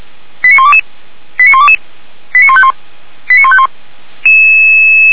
Five-tone selective call (ZVEI), heard as quick runs of five tones, tones of about 70 ms. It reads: 89122 89122 89151 89151
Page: SDR/ZVEI